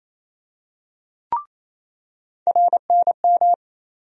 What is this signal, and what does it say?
Morse code, heard as a keyed tone, about 28 wpm, 700 Hz. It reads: RNM
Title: USA Call Sign Postfixes - 3 Letters 28wpm